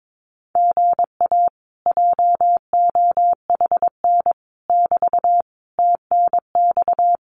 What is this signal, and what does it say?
Morse code, heard as a keyed tone, about 22 wpm, 700 Hz. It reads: GA JOHN <BT> TNX
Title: Common QSO Phrases Encore 22wpm